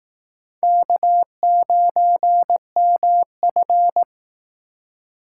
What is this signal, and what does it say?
Morse code, heard as a keyed tone, about 18 wpm, 700 Hz. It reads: K9MF
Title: Speed Racing - USA 1x2 Call Signs 15wpm